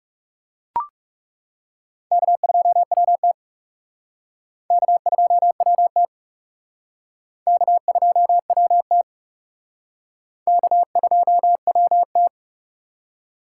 Morse code, heard as a keyed tone, about 40 wpm, 700 Hz. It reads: K2WT K2WT K2WT K2WT